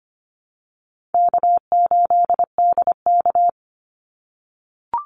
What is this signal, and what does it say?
Morse code, heard as a keyed tone, about 25 wpm, 700 Hz. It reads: K8DK
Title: Call Signs - 25wpm